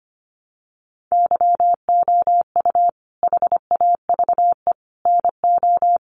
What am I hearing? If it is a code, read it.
Morse code, heard as a keyed tone, about 25 wpm, 700 Hz. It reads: YOU HAVE NO
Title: Sentences from Top 800 Words 25wpm